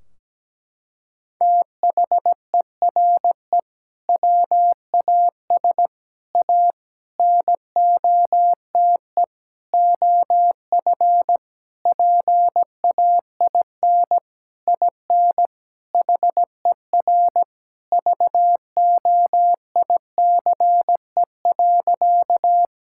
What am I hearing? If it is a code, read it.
Morse code, heard as a keyed tone, about 17 wpm, 700 Hz. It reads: THERE WAS A NOTE OF PAIN IN HER VOICE.
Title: Sentences from Top 1600 Words 17wpm